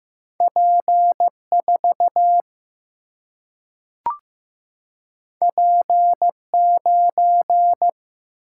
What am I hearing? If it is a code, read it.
Morse code, heard as a keyed tone, about 15 wpm, 700 Hz. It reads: P4 P9
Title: Single-Letter Single-Number 15wpm